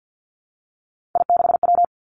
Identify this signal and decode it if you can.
Morse code, heard as a keyed tone, about 50 wpm, 700 Hz. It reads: I6W
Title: Three Challenging Letter-Number Combinations 50wpm